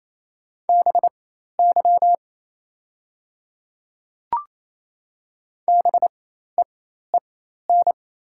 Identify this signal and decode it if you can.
Morse code, heard as a keyed tone, about 27 wpm, spaced as for 13 wpm, 690 Hz. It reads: BY BEEN
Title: 4x Character Spacing - Top 300 Words - 28wpm